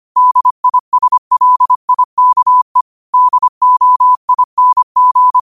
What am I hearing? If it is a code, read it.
Morse code, heard as a keyed tone, about 25 wpm, 1000 Hz. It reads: DISLIKE DOING
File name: 25wpm